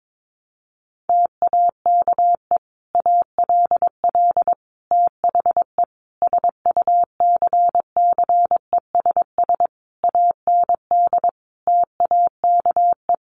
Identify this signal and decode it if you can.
Morse code, heard as a keyed tone, about 22 wpm, 700 Hz. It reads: TAKE ALL THE SUCCESS AND TAKE
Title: Sentences from Top 900 Words 22wpm